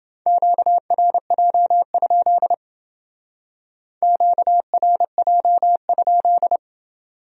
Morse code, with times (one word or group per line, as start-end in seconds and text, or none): 0.3-2.5 QRJ?
4.0-6.6 QRJ?